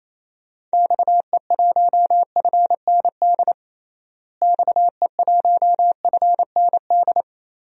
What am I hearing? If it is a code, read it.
Morse code, heard as a keyed tone, about 28 wpm, 700 Hz. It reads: XE1FND XE1FND
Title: DXCC-100 Call Signs - 6 Characters - Repeated 3x - 28wpm